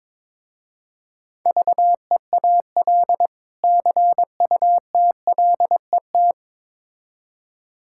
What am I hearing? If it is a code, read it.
Morse code, heard as a keyed tone, about 22 wpm, 700 Hz. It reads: VEAL CUTLET
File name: Difficult Words with Hints - 22wpm